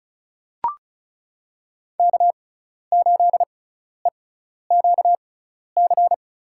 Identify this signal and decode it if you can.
Morse code, heard as a keyed tone, about 36 wpm, spaced as for 12 wpm, 700 Hz. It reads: K8EQC
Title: USA Call Signs - 5 Characters - 6x Character Spacing - 35wpm